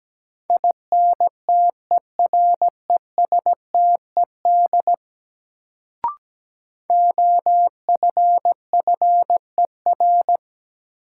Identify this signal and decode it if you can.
Morse code, heard as a keyed tone, about 17 wpm, 700 Hz. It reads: INTERESTED OFFER
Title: Sentences from Top 900 Words - New Word Drill - 17wpm